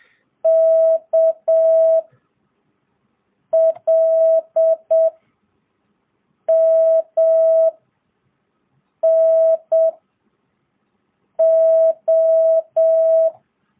Morse code, heard as a keyed tone, about 7 wpm, 650 Hz. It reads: K L M N O
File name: jh8cbh / 2022cw